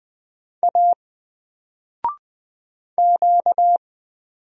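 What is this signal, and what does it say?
Morse code, heard as a keyed tone, about 20 wpm, 700 Hz. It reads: A Q
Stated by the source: Single letter and number Morse drill at 20 wpm, from N4LCD.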